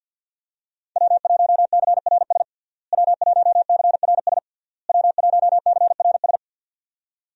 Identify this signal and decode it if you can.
Morse code, heard as a keyed tone, about 50 wpm, 700 Hz. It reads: W1CRS W1CRS W1CRS
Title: USA Call Signs - 5 Characters - Repeated 3x - 50wpm